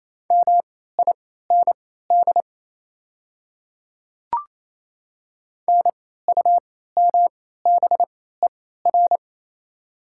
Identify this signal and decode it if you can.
Morse code, heard as a keyed tone, about 28 wpm, spaced as for 16 wpm, 700 Hz. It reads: MIND NUMBER
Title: 3x Character Spacing - Top 500 Words - 28wpm